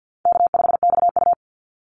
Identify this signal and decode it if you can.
Morse code, heard as a keyed tone, about 50 wpm, 700 Hz. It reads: K5XU
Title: Easy Call Signs Repeated 3x - 50wpm